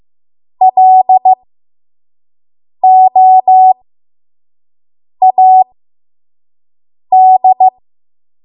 Morse code, heard as a keyed tone, about 15 wpm, spaced as for 5 wpm, 750 Hz. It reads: LOAD
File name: Morse / 05WPM